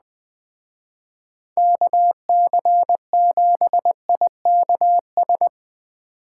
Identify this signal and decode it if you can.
Morse code, heard as a keyed tone, about 20 wpm, 700 Hz. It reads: KC7IKS